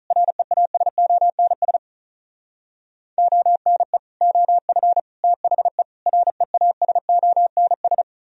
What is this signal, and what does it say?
Morse code, heard as a keyed tone, about 44 wpm, 700 Hz. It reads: REASONS ONE OF THE REASONS